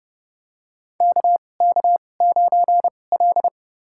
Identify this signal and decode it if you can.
Morse code, heard as a keyed tone, about 31 wpm, spaced as for 22 wpm, 700 Hz. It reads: KK9L
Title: USA Call Signs - 4 Characters - 2x Character Spacing - 30wpm